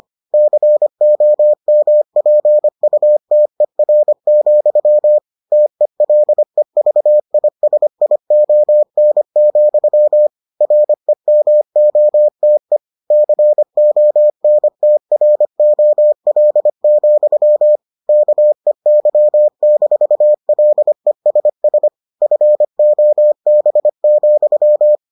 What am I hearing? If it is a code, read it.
Morse code, heard as a keyed tone, about 25 wpm, 600 Hz. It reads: COMPUTER, TELEVISION, REMOTE CONTROL, KEY-LESS FOB,